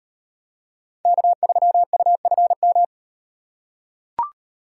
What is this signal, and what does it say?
Morse code, heard as a keyed tone, about 38 wpm, 700 Hz. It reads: K3UFM